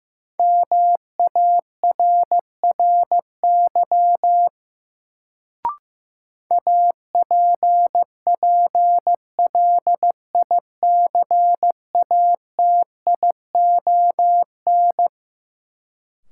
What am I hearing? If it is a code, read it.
Morse code, heard as a keyed tone, about 15 wpm, 700 Hz. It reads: MARRY APPLICATION
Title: Sentences from Top 1200 Words - New Word Drill - 15wpm